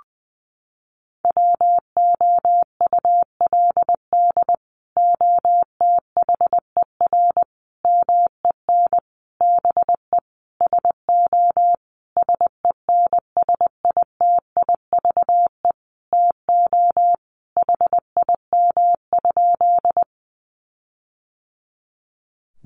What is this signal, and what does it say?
Morse code, heard as a keyed tone, about 20 wpm, 700 Hz. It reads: WOULD OTHER MEN BE SO SENSITIVE TO HIM?
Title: Sentences from Top 1400 Words 20wpm